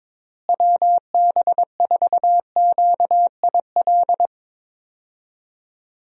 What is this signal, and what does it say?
Morse code, heard as a keyed tone, about 21 wpm, 700 Hz. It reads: WB4QIL
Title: USA Call Signs - 6 Characters - Repeated 3x - 22wpm